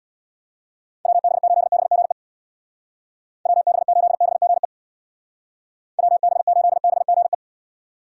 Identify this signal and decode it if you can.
Morse code, heard as a keyed tone, about 60 wpm, 700 Hz. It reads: WB8BGE WB8BGE WB8BGE
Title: Speed Racing - USA 2x3 Call Signs 50wpm